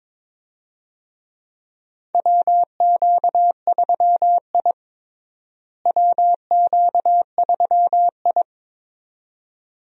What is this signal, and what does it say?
Morse code, heard as a keyed tone, about 22 wpm, 700 Hz. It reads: WQ3I WQ3I
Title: USA Call Signs - 4 Characters - Repeated 2x - 22wpm